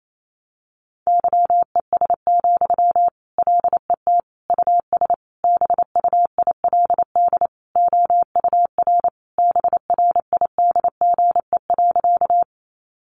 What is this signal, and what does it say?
Morse code, heard as a keyed tone, about 28 wpm, 700 Hz. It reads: YES, LET US BUILD OUR BRIDGE.
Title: Sentences from Top 1200 Words 28wpm